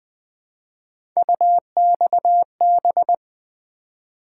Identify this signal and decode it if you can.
Morse code, heard as a keyed tone, about 21 wpm, 700 Hz. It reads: UXB
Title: Three Letter-Number Combinations - 20wpm